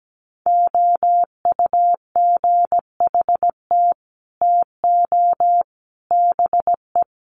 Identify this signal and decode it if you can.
Morse code, heard as a keyed tone, about 17 wpm, 700 Hz. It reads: OUGHT TO BE
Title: Sets of 5 Words Encore2 17wpm